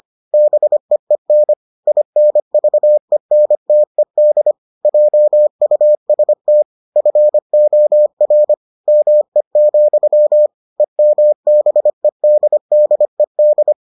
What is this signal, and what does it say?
Morse code, heard as a keyed tone, about 25 wpm, 600 Hz. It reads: BEEN INVENTED JUST FOR ME, EMBEDDED